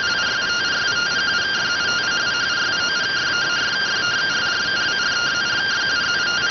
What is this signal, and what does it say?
Two-tone teleprinter signal, about 170 Hz shift.